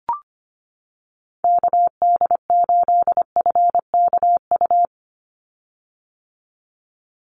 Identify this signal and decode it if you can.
Morse code, heard as a keyed tone, about 25 wpm, 700 Hz. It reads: KD8FKU
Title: USA Call Signs 25wpm